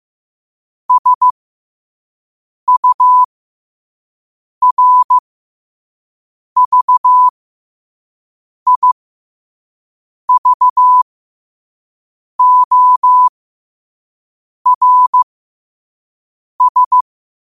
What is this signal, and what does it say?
Morse code, heard as a keyed tone, about 15 wpm, spaced as for 5 wpm, 1000 Hz. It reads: SURVIVORS
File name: qotdfiles / 5wpm